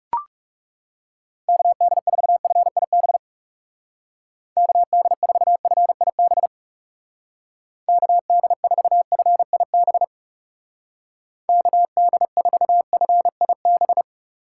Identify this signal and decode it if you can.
Morse code, heard as a keyed tone, about 35 wpm, 700 Hz. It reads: KD4FIB KD4FIB KD4FIB KD4FIB
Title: Speed Racing - USA 2x3 Call Signs 30wpm